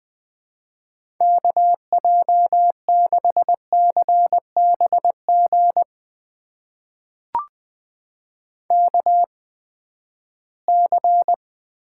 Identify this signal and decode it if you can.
Morse code, heard as a keyed tone, about 21 wpm, spaced as for 14 wpm, 700 Hz. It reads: KJ6CBG KC
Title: USA Call Signs - 6 Characters - 8x Character Spacing - 20wpm